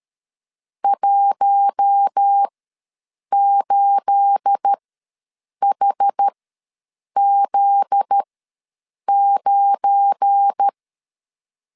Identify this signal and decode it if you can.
Morse code, heard as a keyed tone, about 13 wpm, spaced as for 7 wpm, 800 Hz. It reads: 18HZ9
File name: LEZIONI / 9 28 ws 12wpm 3354